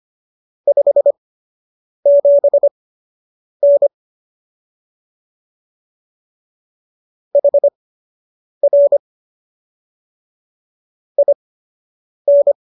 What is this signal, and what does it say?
Morse code, heard as a keyed tone, about 25 wpm, spaced as for 8 wpm, 570 Hz. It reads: 57N HR IN